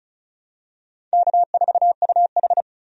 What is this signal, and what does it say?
Morse code, heard as a keyed tone, about 36 wpm, 700 Hz. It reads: K4UH
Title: USA Call Signs - 4 Characters - 35wpm